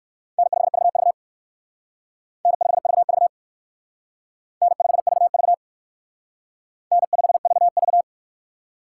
Morse code, heard as a keyed tone, about 56 wpm, 700 Hz. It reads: N5VV N5VV N5VV N5VV
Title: Speed Racing - USA 1x2 Call Signs 45wpm